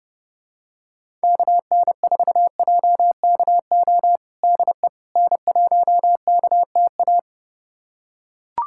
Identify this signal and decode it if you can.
Morse code, heard as a keyed tone, about 32 wpm, 700 Hz. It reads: KN4JKO DE N1KTA
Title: USA Callsign DE Callsign 30wpm